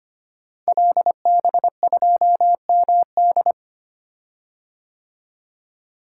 Morse code, heard as a keyed tone, about 25 wpm, 700 Hz. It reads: LB2MD
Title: Code Groups - 5 Letter:Numbers - 25wpm